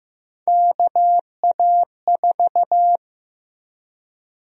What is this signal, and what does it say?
Morse code, heard as a keyed tone, about 15 wpm, 700 Hz. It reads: KA4
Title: USA Call Sign Prefixes - 15wpm